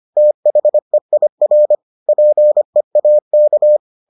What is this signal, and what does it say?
Morse code, heard as a keyed tone, about 25 wpm, 600 Hz. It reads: THEIR PEAK